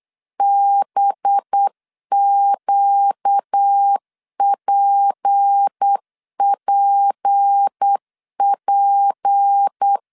Morse code, heard as a keyed tone, about 8 wpm, 800 Hz. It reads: BQPPP
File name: LEZIONI / 6 PKX0 / 6 P 8wpm 3025